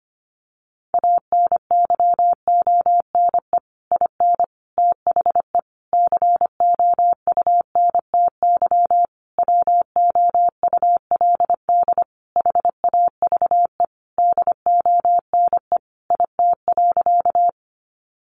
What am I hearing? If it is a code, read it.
Morse code, heard as a keyed tone, about 25 wpm, 700 Hz. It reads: ANYONE IN THE COUNTY WOULD HAVE DONE IT.
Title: Sentences from Top 1200 Words 25wpm